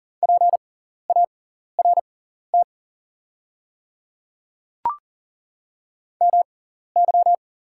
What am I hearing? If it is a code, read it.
Morse code, heard as a keyed tone, about 40 wpm, spaced as for 14 wpm, 690 Hz. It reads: PART MY
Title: 6x Character Spacing - Top 200 Words - 40wpm